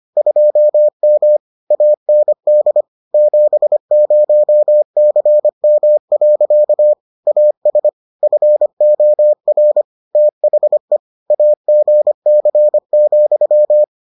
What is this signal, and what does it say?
Morse code, heard as a keyed tone, about 25 wpm, 600 Hz. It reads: 2M AND 70CM. AS FOR THE AGC,